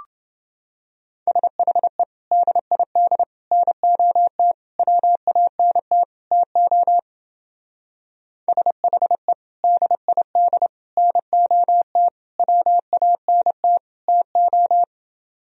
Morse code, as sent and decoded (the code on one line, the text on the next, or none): ... .... / -.. .. -.. / -. --- - / .-- .- -. - / - --- / ... .... / -.. .. -.. / -. --- - / .-- .- -. - / - ---
SHE DID NOT WANT TO SHE DID NOT WANT TO